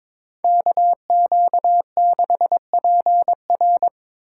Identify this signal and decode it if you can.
Morse code, heard as a keyed tone, about 26 wpm, 700 Hz. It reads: KQ6PR
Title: Speed Racing - USA 2x2 Call Signs 22wpm